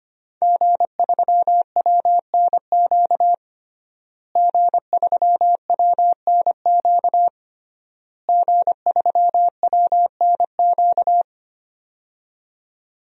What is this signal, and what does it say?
Morse code, heard as a keyed tone, about 25 wpm, 700 Hz. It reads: G3WNQ G3WNQ G3WNQ
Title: DXCC-100 Call Signs - 5 Characters - Repeated 3x - 25wpm